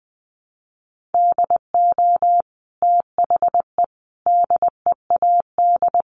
Morse code, heard as a keyed tone, about 20 wpm, 700 Hz. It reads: DO THE DEAD
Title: Sentences from Top 1500 Words 20wpm